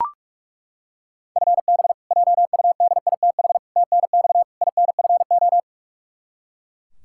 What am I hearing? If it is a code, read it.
Morse code, heard as a keyed tone, about 41 wpm, 700 Hz. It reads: FB JUDITH TNX INFO
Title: K1USN SST QSO Practice 45wpm